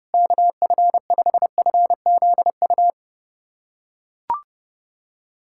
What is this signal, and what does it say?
Morse code, heard as a keyed tone, about 30 wpm, 700 Hz. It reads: KF5FZU
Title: USA Callsign DE Callsign 30wpm